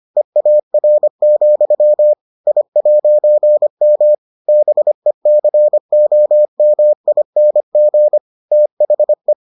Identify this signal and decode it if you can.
Morse code, heard as a keyed tone, about 25 wpm, 600 Hz. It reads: EAR, I'M BECOMING THE